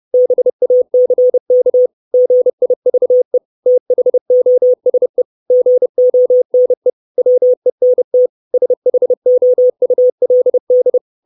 Morse code, heard as a keyed tone, about 30 wpm, 500 Hz. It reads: BACK GIVE THOSE GONE WENT SHOULD
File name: 632 Words 30WPM